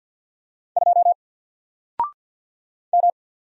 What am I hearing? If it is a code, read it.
Morse code, heard as a keyed tone, about 54 wpm, 700 Hz. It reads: 2 M
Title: Single Letter-Number Alt 50wpm